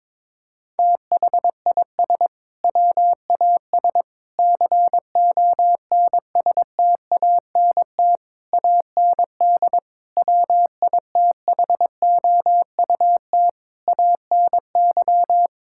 Morse code, heard as a keyed tone, about 22 wpm, 700 Hz. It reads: THIS WAS CONSTANT AND WITHOUT ANY
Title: Sentences from Top 1600 Words 22wpm